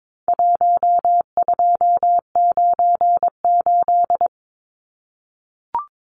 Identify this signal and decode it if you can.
Morse code, heard as a keyed tone, about 22 wpm, 700 Hz. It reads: 1298
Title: Numbers - 4-Digit 22wpm